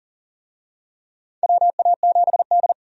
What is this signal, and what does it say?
Morse code, heard as a keyed tone, about 40 wpm, 700 Hz. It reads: WA7D
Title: USA Call Signs - 4 Characters - 40wpm